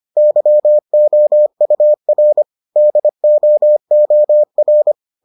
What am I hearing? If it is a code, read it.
Morse code, heard as a keyed tone, about 25 wpm, 600 Hz. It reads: YOUR DOOR